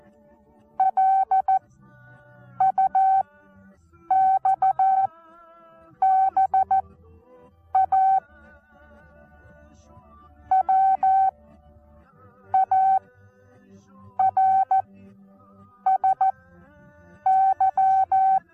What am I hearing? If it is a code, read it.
Morse code, heard as a keyed tone, about 14 wpm, spaced as for 6 wpm, 770 Hz. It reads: LUXBA WARSY
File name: Onlinelehrgang / morsen / DJ4UF Morsen19